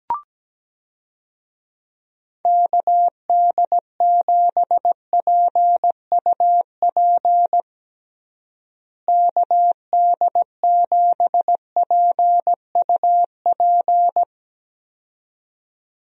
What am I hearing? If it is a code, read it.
Morse code, heard as a keyed tone, about 17 wpm, 700 Hz. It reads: KD7PUP KD7PUP
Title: USA Call Signs - 6 Characters - Repeated 2x - 17wpm